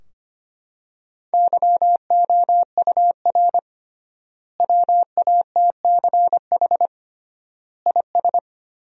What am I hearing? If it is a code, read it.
Morse code, heard as a keyed tone, about 25 wpm, 700 Hz. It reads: YOUR WATCH IS
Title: 3x Word-Spacing - Sentences from Top 400 Words - 25wpm